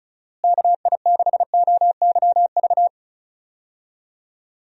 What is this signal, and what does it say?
Morse code, heard as a keyed tone, about 36 wpm, 700 Hz. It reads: KI6OYV
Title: USA Call Signs - 6 Characters - Repeated 2x - 35wpm